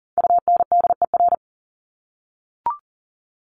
Morse code, heard as a keyed tone, about 40 wpm, 700 Hz. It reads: UNDER
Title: Top 200 Words Repeated 40wpm